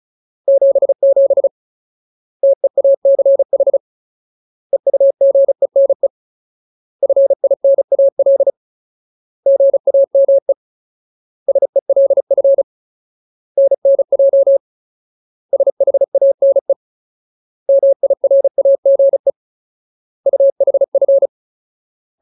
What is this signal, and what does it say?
Morse code, heard as a keyed tone, about 34 wpm, 550 Hz. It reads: 77 TEACH EUGENE FINAL GAME SELF NNJ SHANE MIRAGE UHF
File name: Ultimate Word List / 35WPM Ultimate